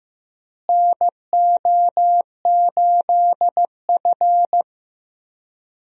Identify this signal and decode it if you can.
Morse code, heard as a keyed tone, about 15 wpm, 700 Hz. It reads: NO8F
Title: USA Call Signs - 4 Characters - Repeated 3x - 15wpm